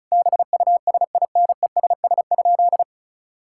Morse code, heard as a keyed tone, about 36 wpm, 700 Hz. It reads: BUSINESS?
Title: Sentences from Top 1000 Words - Review Part 6 - 35wpm